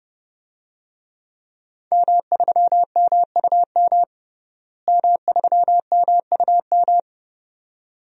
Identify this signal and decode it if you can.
Morse code, heard as a keyed tone, about 30 wpm, 700 Hz. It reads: M3MUM M3MUM
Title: DXCC-100 Call Signs - 5 Characters - Repeated 2x - 30wpm